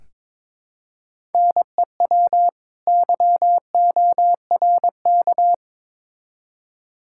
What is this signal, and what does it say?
Morse code, heard as a keyed tone, about 22 wpm, 700 Hz. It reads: NEW YORK
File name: Build A Phrase - 2-Word Phrases - 22wpm